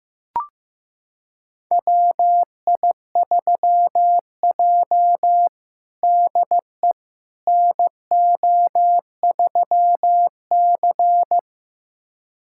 Morse code, heard as a keyed tone, about 15 wpm, 700 Hz. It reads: WI3J DE NO3C
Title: USA Callsign DE Callsign - 4 Characters - 15wpm